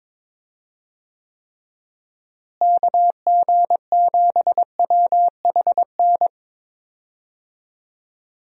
Morse code, heard as a keyed tone, about 22 wpm, 700 Hz. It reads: KG7WHN
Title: USA Call Signs - 6 Characters - Repeated 2x - 22wpm